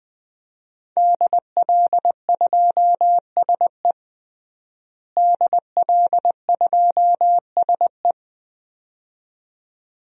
Morse code, heard as a keyed tone, about 21 wpm, 700 Hz. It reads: DL2SE DL2SE
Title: DXCC-100 Call Signs - 5 Characters - Repeated 2x - 20wpm